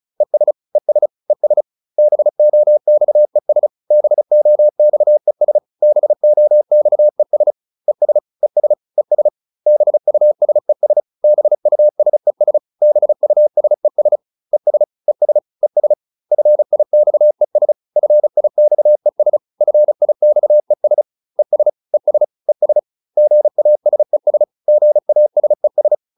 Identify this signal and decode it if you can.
Morse code, heard as a keyed tone, about 34 wpm, 600 Hz. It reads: ES ES ES BOXES BOXES BOXES ES ES ES BUSES BUSES BUSES ES ES ES FIXES FIXES FIXES ES ES ES GASES GASES